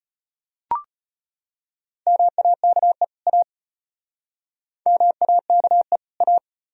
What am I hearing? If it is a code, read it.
Morse code, heard as a keyed tone, about 36 wpm, 700 Hz. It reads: MAKE A MAKE A